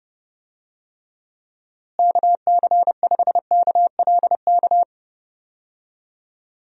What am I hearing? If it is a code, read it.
Morse code, heard as a keyed tone, about 30 wpm, 700 Hz. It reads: KC5KLK
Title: USA Call Signs - 6 Characters - Repeated 2x - 30wpm